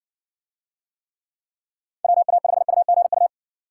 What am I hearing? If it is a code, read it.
Morse code, heard as a keyed tone, about 60 wpm, 700 Hz. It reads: WA5UGU